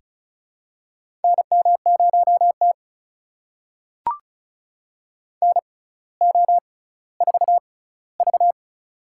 Morse code, heard as a keyed tone, about 36 wpm, spaced as for 18 wpm, 700 Hz. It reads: NM0T NO4V
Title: USA Call Signs - 4 Characters - 6x Character Spacing - 35wpm